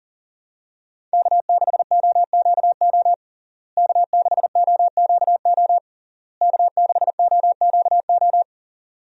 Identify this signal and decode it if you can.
Morse code, heard as a keyed tone, about 40 wpm, 700 Hz. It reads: K6OQO K6OQO K6OQO
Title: USA Call Signs - 5 Characters - Repeated 3x - 40wpm